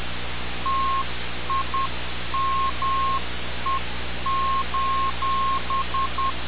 Morse code, heard as a keyed tone, about 10 wpm, 1070 Hz. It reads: TIME:
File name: radiocap / REC z